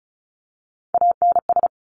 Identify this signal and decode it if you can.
Morse code, heard as a keyed tone, about 36 wpm, 700 Hz. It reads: ANS
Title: Top Three Letter Combinations in Words 35wpm